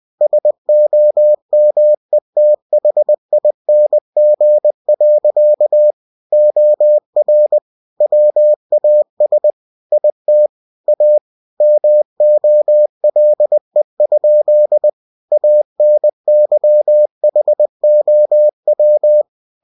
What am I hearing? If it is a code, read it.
Morse code, heard as a keyed tone, about 20 wpm, 600 Hz. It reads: SOMETHING. OR WAS IT A MOLE? ANYHOW